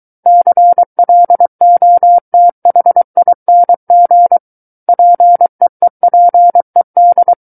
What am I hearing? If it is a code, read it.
Morse code, heard as a keyed tone, about 23 wpm, 700 Hz. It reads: CLOTHING PEEPED
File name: Book One / 14-Chapter 14 - 23 WPM